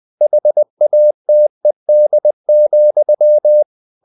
Morse code, heard as a keyed tone, about 20 wpm, 600 Hz. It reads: HATED,
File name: CW / Practice / If20